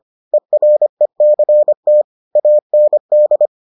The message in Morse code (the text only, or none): ERECT AND